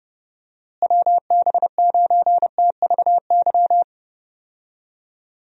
Morse code, heard as a keyed tone, about 32 wpm, 700 Hz. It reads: WB9TVY
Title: USA Call Signs - 6 Characters - Repeated 2x - 30wpm